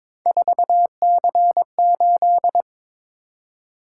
Morse code, heard as a keyed tone, about 22 wpm, 700 Hz. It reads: 4C8
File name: practice / 22wpm / Three Letter-Number Combinations - 22wpm